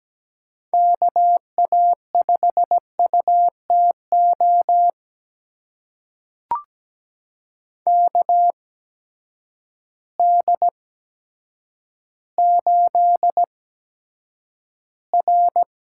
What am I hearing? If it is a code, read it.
Morse code, heard as a keyed tone, about 17 wpm, spaced as for 8 wpm, 700 Hz. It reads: KA5UTO KD8R
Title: USA Call Signs - 6 Characters - 8x Character Spacing - 17wpm